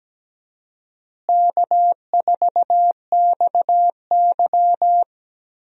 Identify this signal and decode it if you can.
Morse code, heard as a keyed tone, about 17 wpm, 700 Hz. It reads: K4XY
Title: USA Call Signs - 4 Characters - Repeated 3x - 17wpm